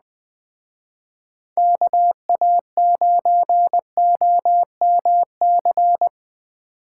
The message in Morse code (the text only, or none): KA9OMC